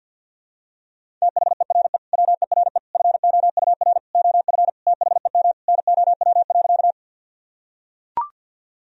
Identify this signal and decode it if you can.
Morse code, heard as a keyed tone, about 45 wpm, 700 Hz. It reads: THERE WERE FOUR OF THEM NOW.
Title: Sentences from Top 1000 Words - Review Part 1 - 50wpm